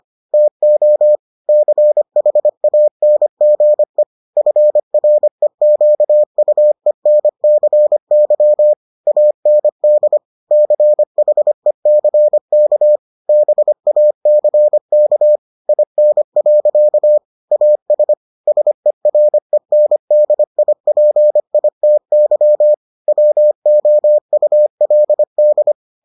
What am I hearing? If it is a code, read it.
Morse code, heard as a keyed tone, about 25 wpm, 600 Hz. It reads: TO CHANGE FREQUENCY AND CHECK BACK IN. AS SERENDIPITY WOULD